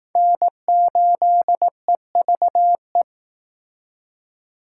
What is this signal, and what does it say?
Morse code, heard as a keyed tone, about 20 wpm, 700 Hz. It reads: N8EVE